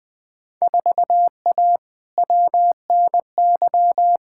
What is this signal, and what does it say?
Morse code, heard as a keyed tone, about 20 wpm, 700 Hz. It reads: 4A WNY
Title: Field Day Exchanges Encore - 20wpm